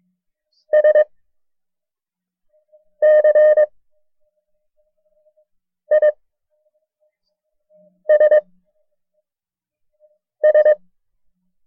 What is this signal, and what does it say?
Morse code, heard as a keyed tone, about 22 wpm, 610 Hz. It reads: S C I S S
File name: FreeFiles / CW-practice / KB6MT-tape1-side2